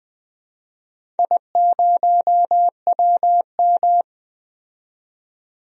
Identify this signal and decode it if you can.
Morse code, heard as a keyed tone, about 20 wpm, 700 Hz. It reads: I0WM
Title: DXCC-100 Call Signs - 4 Characters - 20wpm